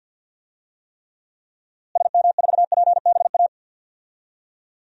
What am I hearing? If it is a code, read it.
Morse code, heard as a keyed tone, about 45 wpm, 700 Hz. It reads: SM4PBA
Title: DXCC-100 Call Signs - 6 Characters - Repeated 3x - 50wpm